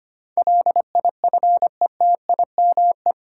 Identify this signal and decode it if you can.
Morse code, heard as a keyed tone, about 25 wpm, 700 Hz. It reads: LIFETIME
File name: Eight Letter Words - 25wpm